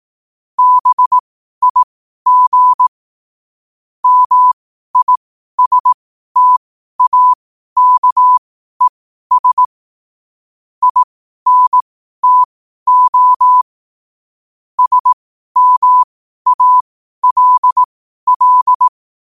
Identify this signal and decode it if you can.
Morse code, heard as a keyed tone, about 18 wpm, spaced as for 12 wpm, 1000 Hz. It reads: BIG MISTAKES INTO SMALL
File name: qotdfiles / 12wpm